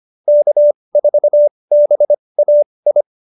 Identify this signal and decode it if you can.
Morse code, heard as a keyed tone, about 25 wpm, spaced as for 20 wpm, 590 Hz. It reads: K4BAI
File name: QSO301 20